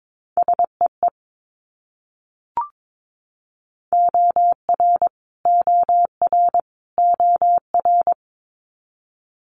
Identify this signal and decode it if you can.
Morse code, heard as a keyed tone, about 22 wpm, 700 Hz. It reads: SEE OR OR OR
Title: Multi-letter Phonemes Repeated 3x 22wpm